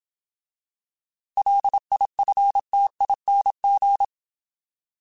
Morse code, heard as a keyed tone, about 27 wpm, 800 Hz. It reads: LIFTING